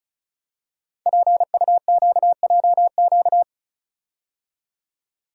Morse code, heard as a keyed tone, about 36 wpm, 700 Hz. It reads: PUQJQ
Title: Code Groups - 5 Letters - 35wpm